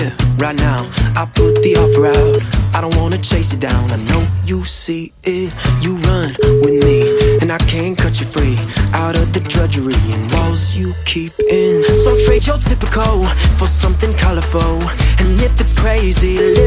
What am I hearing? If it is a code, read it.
Ringback tone (Europe, 425 Hz), with noise.